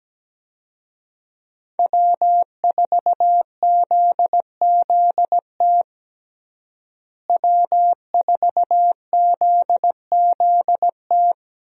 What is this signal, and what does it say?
Morse code, heard as a keyed tone, about 17 wpm, 700 Hz. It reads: W4ZZT W4ZZT
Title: USA Call Signs - 5 Characters - Repeated 3x - 17wpm